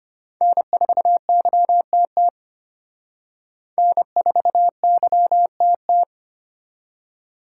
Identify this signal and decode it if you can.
Morse code, heard as a keyed tone, about 31 wpm, 700 Hz. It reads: N4YTT N4YTT
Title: Speed Racing - USA 1x3 Call Signs 25wpm